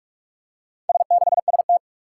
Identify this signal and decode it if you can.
Morse code, heard as a keyed tone, about 44 wpm, 700 Hz. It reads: S6ST
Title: Four Challenging Letter-Number Combinations 45wpm